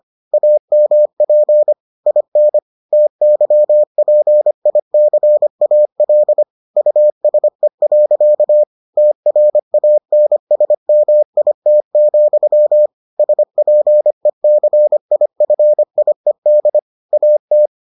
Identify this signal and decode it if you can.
Morse code, heard as a keyed tone, about 25 wpm, 600 Hz. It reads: AMP IN TYPICAL USE. TRANSMIT, SPECIFIED AT